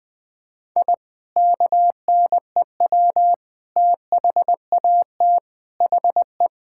Morse code, heard as a keyed tone, about 21 wpm, 700 Hz. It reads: I KNEW THAT HE
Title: Sentences with 3-Word Phrases - Spoken Beginning - 20wpm